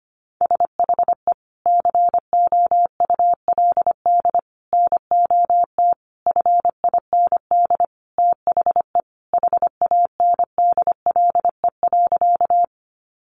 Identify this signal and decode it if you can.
Morse code, heard as a keyed tone, about 25 wpm, 700 Hz. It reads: SHE COULD NOT FIND THE HANDLE.
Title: Sentences from Top 1100 Words 25wpm